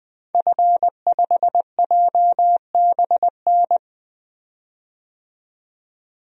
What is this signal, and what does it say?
Morse code, heard as a keyed tone, about 20 wpm, 700 Hz. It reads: F5JBN
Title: DXCC-100 Call Signs - 5 Characters - 20wpm